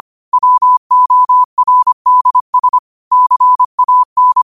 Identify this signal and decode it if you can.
Morse code, heard as a keyed tone, about 25 wpm, 1000 Hz. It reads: WORDS CAN
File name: qotdfiles / 25wpm